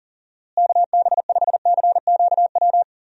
Morse code, heard as a keyed tone, about 37 wpm, 700 Hz. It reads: KB5CQW
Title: USA Call Signs - 6 Characters - Repeated 2x - 40wpm